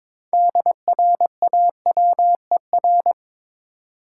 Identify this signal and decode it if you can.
Morse code, heard as a keyed tone, about 22 wpm, 700 Hz. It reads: DRAWER
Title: Six Letter Words - 22wpm